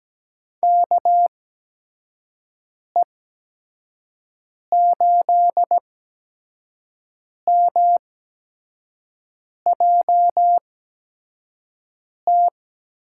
Morse code, heard as a keyed tone, about 17 wpm, spaced as for 5 wpm, 700 Hz. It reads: KE8MJT
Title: USA Call Signs - 6 Characters - 8x Character Spacing - 17wpm